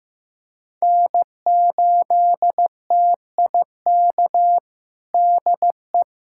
Morse code, heard as a keyed tone, about 14 wpm, 700 Hz. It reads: N8TIK DE
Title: USA Callsign DE Callsign - 5 Characters - 15wpm